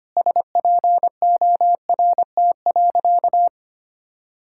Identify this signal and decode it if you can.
Morse code, heard as a keyed tone, about 25 wpm, 700 Hz. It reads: SPORT.
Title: Sentences from Top 1600 Words 25wpm